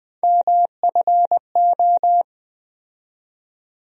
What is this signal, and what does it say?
Morse code, heard as a keyed tone, about 20 wpm, 700 Hz. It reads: MFO
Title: USA Call Sign Postfixes - 3 Letters 20wpm